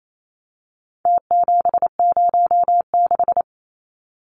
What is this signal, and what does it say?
Morse code, heard as a keyed tone, about 28 wpm, 700 Hz. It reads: T706
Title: Numbers - Contest Style - 4-Digit 28wpm